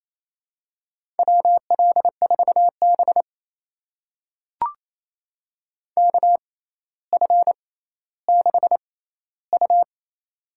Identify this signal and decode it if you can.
Morse code, heard as a keyed tone, about 28 wpm, spaced as for 14 wpm, 700 Hz. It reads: WL4B KF6U
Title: USA Call Signs - 6x Character Spacing - 28wpm